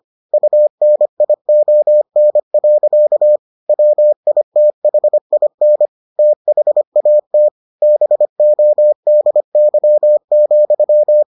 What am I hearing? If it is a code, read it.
Morse code, heard as a keyed tone, about 25 wpm, 600 Hz. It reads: UNION. WITHIN THAT BODY,